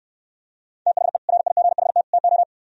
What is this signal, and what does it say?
Morse code, heard as a keyed tone, about 66 wpm, 700 Hz. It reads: THE RIGHT TO